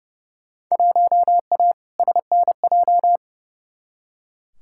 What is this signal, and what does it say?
Morse code, heard as a keyed tone, about 30 wpm, 700 Hz. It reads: 1A SNJ
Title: Field Day Exchanges - 30wpm